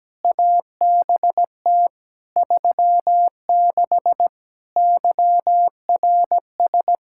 Morse code, heard as a keyed tone, about 17 wpm, 700 Hz. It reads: ABT 36 YRS
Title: Practice QSO - Exchange 3 - More Info - 17wpm